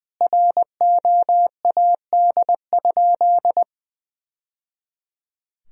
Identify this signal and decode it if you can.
Morse code, heard as a keyed tone, about 20 wpm, 700 Hz. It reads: ROAD?
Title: Sentences from Top 1000 Words - Review Part 2 - 20wpm